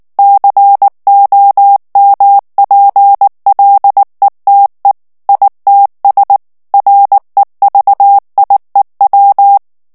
Morse code, heard as a keyed tone, about 19 wpm, 800 Hz. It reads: COMPLETE ITS REVIEW
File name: ANGLAIS texte clair 3 vitesse 18 mots minute